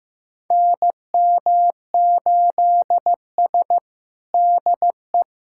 Morse code, heard as a keyed tone, about 15 wpm, 700 Hz. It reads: NM8S DE
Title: USA Callsign DE Callsign - 4 Characters - 15wpm